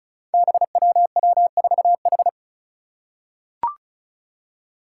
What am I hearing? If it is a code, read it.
Morse code, heard as a keyed tone, about 36 wpm, 700 Hz. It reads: BWW4H
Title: Five Challenging Letter-Number Combinations 35wpm